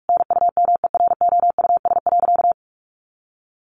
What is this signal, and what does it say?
Morse code, heard as a keyed tone, about 44 wpm, 700 Hz. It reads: NUMEROUS.
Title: Sentences from Top 1400 Words 45wpm